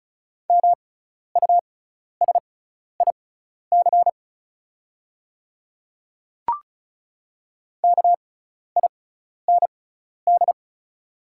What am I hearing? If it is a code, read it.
Morse code, heard as a keyed tone, about 36 wpm, spaced as for 12 wpm, 700 Hz. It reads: MUSIC KIND
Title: 6x Character Spacing - Top 500 Words - 35wpm